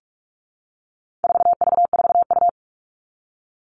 Morse code, heard as a keyed tone, about 44 wpm, 700 Hz. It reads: 4V4U
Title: All Cut Numbers - 4-Digit 45wpm